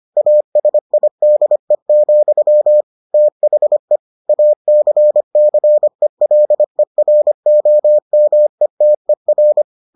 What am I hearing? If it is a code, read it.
Morse code, heard as a keyed tone, about 25 wpm, 600 Hz. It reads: ASIDE, THE ACCELEROMETER